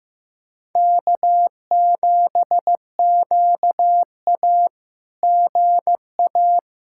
Morse code, heard as a keyed tone, about 15 wpm, 700 Hz. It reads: K7QA GA
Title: SOTA QSO Practice 15wpm